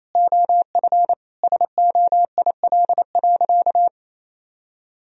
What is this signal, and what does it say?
Morse code, heard as a keyed tone, about 27 wpm, 700 Hz. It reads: OF SOIL.
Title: Sentences from Top 1000 Words - Review Part 4 - 28wpm